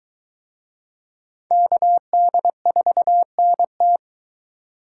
Morse code, heard as a keyed tone, about 23 wpm, 700 Hz. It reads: KD4NT